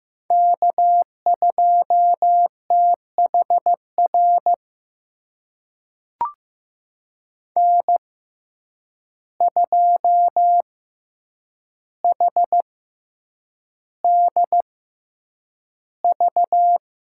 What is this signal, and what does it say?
Morse code, heard as a keyed tone, about 15 wpm, spaced as for 8 wpm, 700 Hz. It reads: K2THR N2HDV
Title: USA Call Signs - 5 Characters - 6x Character Spacing - 15wpm